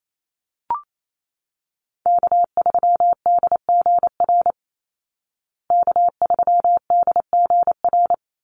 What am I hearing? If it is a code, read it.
Morse code, heard as a keyed tone, about 28 wpm, 700 Hz. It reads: K3DGR K3DGR
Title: Call Signs Repeated 2x - 28wpm